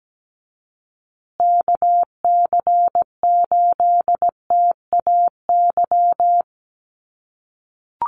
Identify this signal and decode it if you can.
Morse code, heard as a keyed tone, about 17 wpm, 700 Hz. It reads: KC8TAY
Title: USA Call Signs Repeated 2x 17wpm